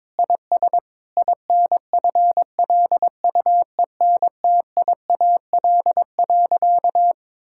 Morse code, heard as a keyed tone, about 22 wpm, 700 Hz. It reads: IS INFLUENTIAL.
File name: practice / 22wpm / Sentences with 4-Word Phrases - 22wpm